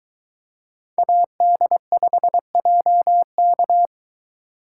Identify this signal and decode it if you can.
Morse code, heard as a keyed tone, about 22 wpm, 700 Hz. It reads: AD5JK